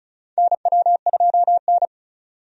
Morse code, heard as a keyed tone, about 34 wpm, 700 Hz. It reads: NW2N
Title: USA Call Signs - 4 Characters - 35wpm